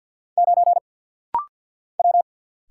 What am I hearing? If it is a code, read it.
Morse code, heard as a keyed tone, about 49 wpm, 700 Hz. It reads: 9 W
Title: Single Letter-Number Alt - Rapid-Fire 50wpm